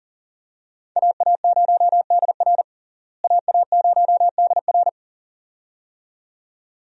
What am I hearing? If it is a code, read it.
Morse code, heard as a keyed tone, about 40 wpm, 700 Hz. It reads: AA0DR AA0DR
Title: USA Call Signs - 5 Characters - Repeated 2x - 40wpm